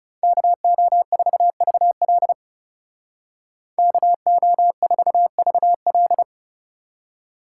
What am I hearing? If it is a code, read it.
Morse code, heard as a keyed tone, about 36 wpm, 700 Hz. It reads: KO4VL KO4VL
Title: Speed Racing - USA 2x2 Call Signs 30wpm